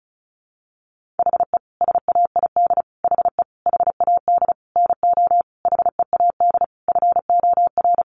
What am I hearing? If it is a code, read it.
Morse code, heard as a keyed tone, about 34 wpm, 700 Hz. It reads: HE SAID HE HAD NO HEAD FOR
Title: Sentences from Top 1500 Words 35wpm